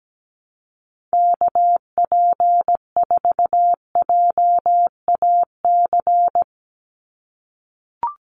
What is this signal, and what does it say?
Morse code, heard as a keyed tone, about 17 wpm, 700 Hz. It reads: KP4JAC
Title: USA Call Signs 17wpm